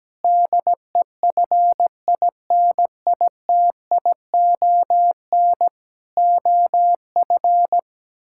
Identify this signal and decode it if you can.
Morse code, heard as a keyed tone, about 17 wpm, 700 Hz. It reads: DEFINITION OF
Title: Sentences from Top 1000 Words - Review Part 6 - 17wpm